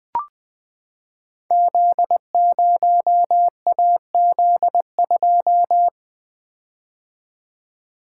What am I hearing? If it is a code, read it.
Morse code, heard as a keyed tone, about 20 wpm, 700 Hz. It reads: Z0AZ2
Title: Code Groups - 5 Letter:Numbers - 20wpm